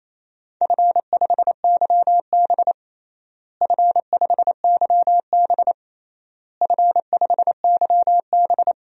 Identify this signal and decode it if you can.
Morse code, heard as a keyed tone, about 28 wpm, 700 Hz. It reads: F5YB F5YB F5YB
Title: DXCC-100 Call Signs - 4 Characters - Repeated 3x - 28wpm